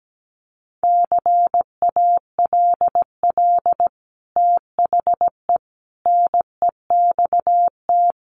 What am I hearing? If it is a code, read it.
Morse code, heard as a keyed tone, about 17 wpm, 700 Hz. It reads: CALL THE NEXT
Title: Sentences from Top 300 Words 17wpm